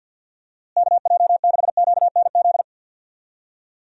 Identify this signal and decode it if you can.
Morse code, heard as a keyed tone, about 51 wpm, 700 Hz. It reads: KJ6QNZ
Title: USA Call Signs - 6 Characters - Repeated 2x - 50wpm